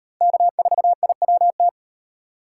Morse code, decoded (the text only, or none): K4IWT